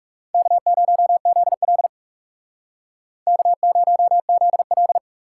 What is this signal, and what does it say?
Morse code, heard as a keyed tone, about 40 wpm, 700 Hz. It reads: K0ZL K0ZL